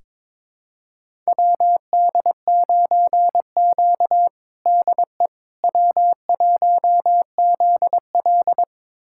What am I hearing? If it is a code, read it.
Morse code, heard as a keyed tone, about 22 wpm, 700 Hz. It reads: WD9Q DE W1ZL
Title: USA Callsign DE Callsign - 4 Characters - 22wpm